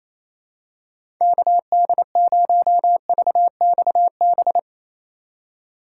Morse code, heard as a keyed tone, about 28 wpm, 700 Hz. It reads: KD0VXB
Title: USA Call Signs - 6 Characters - 2x Character Spacing - 28wpm